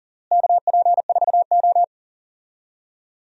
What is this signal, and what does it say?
Morse code, heard as a keyed tone, about 37 wpm, 700 Hz. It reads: KP4O
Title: DXCC-100 Call Signs - 4 Characters - Repeated 3x - 40wpm